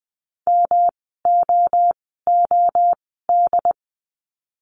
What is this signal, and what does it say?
Morse code, heard as a keyed tone, about 20 wpm, spaced as for 14 wpm, 700 Hz. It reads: MOOD
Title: 108 Lesson 18f - Letter m - Words 20wpm